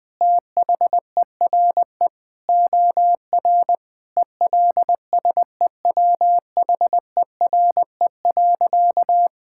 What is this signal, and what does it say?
Morse code, heard as a keyed tone, about 20 wpm, 700 Hz. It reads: THERE OR ELSEWHERE.
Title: Sentences from Top 1600 Words 20wpm